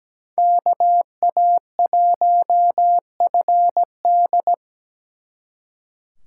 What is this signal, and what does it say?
Morse code, heard as a keyed tone, about 20 wpm, 700 Hz. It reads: KA1FD
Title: Speed Racing - USA 2x2 Call Signs 17wpm